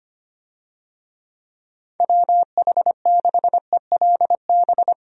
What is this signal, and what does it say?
Morse code, heard as a keyed tone, about 25 wpm, 700 Hz. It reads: WH6ELB
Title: USA Call Signs - 6 Characters - Repeated 3x - 25wpm